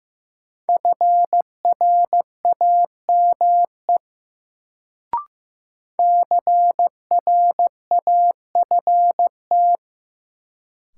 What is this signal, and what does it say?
Morse code, heard as a keyed tone, about 15 wpm, 700 Hz. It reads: FRAME CRAFT
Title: Sentences from Top 1000 Words - Review Part 6 - New Word Drill - 15wpm